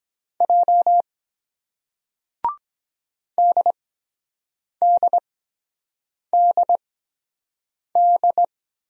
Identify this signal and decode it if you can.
Morse code, heard as a keyed tone, about 23 wpm, spaced as for 7 wpm, 700 Hz. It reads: J DDDD